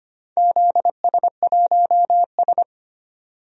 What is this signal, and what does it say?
Morse code, heard as a keyed tone, about 25 wpm, 700 Hz. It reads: ZS1S
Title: DXCC-100 Call Signs - 4 Characters - Repeated 2x - 25wpm